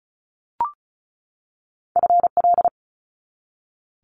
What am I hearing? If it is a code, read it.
Morse code, heard as a keyed tone, about 36 wpm, 700 Hz. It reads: FL
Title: Multi-letter Phonemes 35wpm